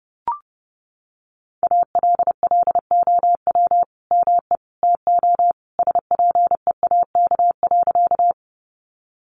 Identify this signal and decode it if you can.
Morse code, heard as a keyed tone, about 30 wpm, 700 Hz. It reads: ALLOW ME TO SPEAK.
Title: Sentences from Top 1100 Words 30wpm